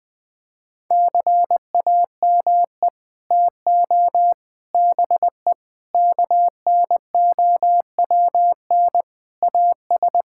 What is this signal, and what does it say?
Morse code, heard as a keyed tone, about 19 wpm, 700 Hz. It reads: CAME TO BE KNOWN AS
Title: Sets of 6 Words - 20wpm